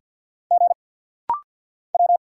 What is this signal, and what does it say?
Morse code, heard as a keyed tone, about 50 wpm, 700 Hz. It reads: G W
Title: Single Letters - Rapid-Fire 50wpm